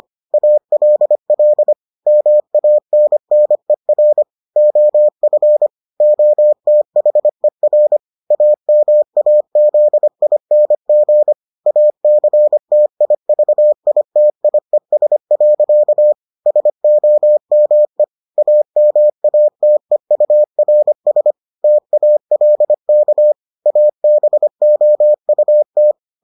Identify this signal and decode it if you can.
Morse code, heard as a keyed tone, about 25 wpm, 600 Hz. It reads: ALL MANNER OF OTHER AMAZING ACTIVITIES. SOME AMATEURS TALK ABOUT